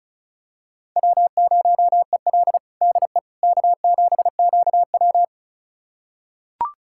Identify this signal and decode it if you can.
Morse code, heard as a keyed tone, about 36 wpm, 700 Hz. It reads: W0EL DE K7QW
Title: USA Callsign DE Callsign - 4 Characters - 35wpm